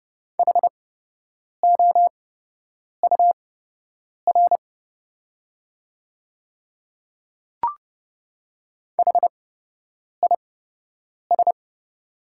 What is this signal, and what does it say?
Morse code, heard as a keyed tone, about 30 wpm, spaced as for 8 wpm, 700 Hz. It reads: HOUR HIS
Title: 8x Character Spacing - Top 500 Words - 30wpm